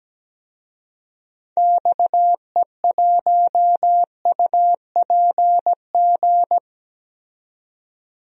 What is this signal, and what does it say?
Morse code, heard as a keyed tone, about 17 wpm, 700 Hz. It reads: XE1UPG
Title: DXCC-100 Call Signs - 6 Characters - 17wpm